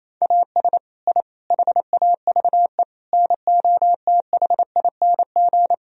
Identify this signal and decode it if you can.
Morse code, heard as a keyed tone, about 28 wpm, 700 Hz. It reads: AS I HAVE NOTHING
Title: Sentences from Top 1000 Words - Review Part 3 - 28wpm